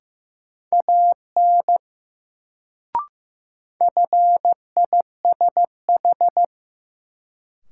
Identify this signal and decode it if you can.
Morse code, heard as a keyed tone, about 16 wpm, 700 Hz. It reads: AN FISH
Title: Sentences from Top 300 Words - New Word Drill - 15wpm